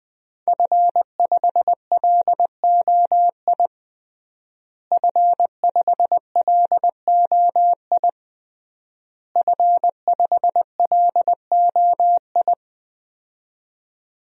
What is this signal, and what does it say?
Morse code, heard as a keyed tone, about 20 wpm, 700 Hz. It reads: F5LOI F5LOI F5LOI
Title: DXCC-100 Call Signs - 5 Characters - Repeated 3x - 20wpm